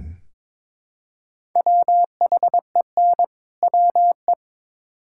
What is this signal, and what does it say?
Morse code, heard as a keyed tone, about 22 wpm, 700 Hz. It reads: WHEN WE
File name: Build A Phrase - 2-Word Phrases - 22wpm